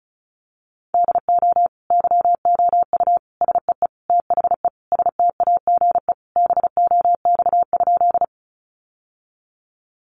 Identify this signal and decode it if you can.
Morse code, heard as a keyed tone, about 34 wpm, 700 Hz. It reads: DO YOU SEE THE STAGE BOX?
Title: Sentences from Top 1200 Words 35wpm